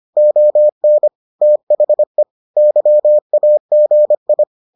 Morse code, heard as a keyed tone, about 25 wpm, 600 Hz. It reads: ON THE YAGI